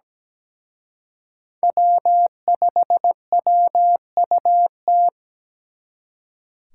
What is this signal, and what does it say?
Morse code, heard as a keyed tone, about 20 wpm, 700 Hz. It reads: W5WUT
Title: Speed Racing - USA 1x3 Call Signs 17wpm